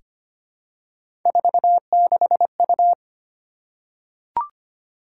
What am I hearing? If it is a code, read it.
Morse code, heard as a keyed tone, about 25 wpm, 700 Hz. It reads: 46U
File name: Three Letter-Number Combinations - 25wpm